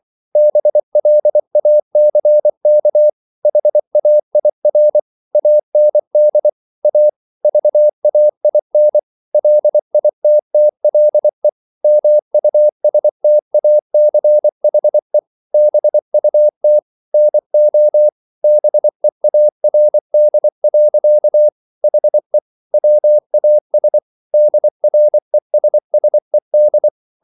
Morse code, heard as a keyed tone, about 24 wpm, 600 Hz. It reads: BLACK HAIR AND A VAIN LITTLE MUSTACHE BUT NO BEARD. HE WAS DRESSED